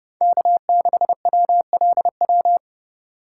Morse code, heard as a keyed tone, about 29 wpm, 700 Hz. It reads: K6WLW